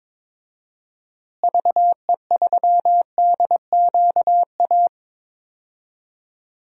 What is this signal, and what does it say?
Morse code, heard as a keyed tone, about 22 wpm, 700 Hz. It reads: VE3DQA
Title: DXCC-100 Call Signs - 6 Characters - Repeated 3x - 22wpm